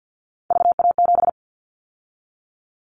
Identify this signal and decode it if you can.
Morse code, heard as a keyed tone, about 50 wpm, 700 Hz. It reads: VA7
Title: Callsign Prefixes for Top 50 DXCC - 50wpm